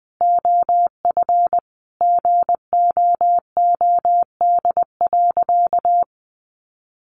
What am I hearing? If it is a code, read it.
Morse code, heard as a keyed tone, about 20 wpm, 700 Hz. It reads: OF GOOD.
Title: Sentences from Top 1300 Words 20wpm